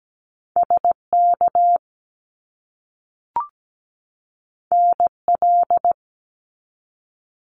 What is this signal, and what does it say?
Morse code, heard as a keyed tone, about 17 wpm, 700 Hz. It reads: SK NL
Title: Canadian Provinces and Territories 17wpm